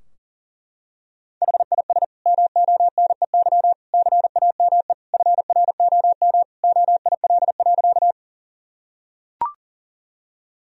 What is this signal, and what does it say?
Morse code, heard as a keyed tone, about 40 wpm, 700 Hz. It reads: HIS MONEY CAME FROM OIL.
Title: Sentences from Top 1000 Words - Review Part 4 - 40wpm